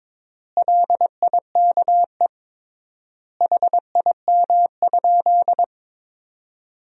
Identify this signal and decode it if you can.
Morse code, heard as a keyed tone, about 22 wpm, 700 Hz. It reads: LIKE HIM?
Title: 3x Word-Spacing - Sentences from Top 100 Words - 22wpm